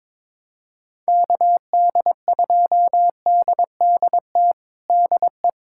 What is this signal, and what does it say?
Morse code, heard as a keyed tone, about 21 wpm, 700 Hz. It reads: KD2DDT DE
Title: USA Callsign DE Callsign - 6 Characters - 22wpm